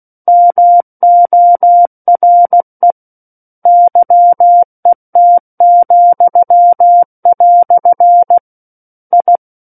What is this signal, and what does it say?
Morse code, heard as a keyed tone, about 16 wpm, 700 Hz. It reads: MORE YET," I
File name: Book One / 07-Chapter 7 - 16 WPM